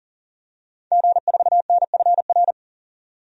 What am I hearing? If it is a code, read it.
Morse code, heard as a keyed tone, about 40 wpm, 700 Hz. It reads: G4NFR
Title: DXCC-100 Call Signs - 5 Characters - 40wpm